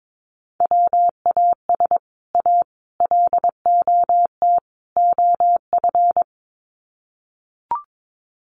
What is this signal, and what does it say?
Morse code, heard as a keyed tone, about 22 wpm, 700 Hz. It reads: WAS A LOT OF
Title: Sets of 4 Words 22wpm